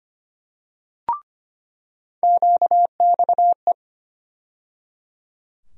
Morse code, heard as a keyed tone, about 25 wpm, 700 Hz. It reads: QXE
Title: USA Call Sign Postfixes - 3 Letters 25wpm